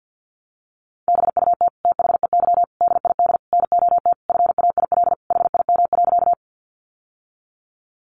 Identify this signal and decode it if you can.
Morse code, heard as a keyed tone, about 48 wpm, 700 Hz. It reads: BUT THEY DID NOT FAIL HIM.
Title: Sentences from Top 1100 Words 50wpm